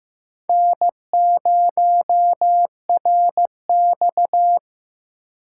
Morse code, heard as a keyed tone, about 18 wpm, 700 Hz. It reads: N0RX
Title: Speed Racing - USA 1x2 Call Signs 15wpm